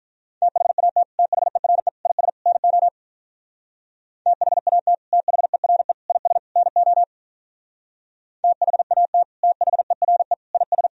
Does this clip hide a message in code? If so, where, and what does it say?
Yes: Morse code from 0.4 to 10.9 s, heard as a keyed tone, about 46 wpm, 700 Hz. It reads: THAT THERE IS NO THAT THERE IS NO THAT THERE IS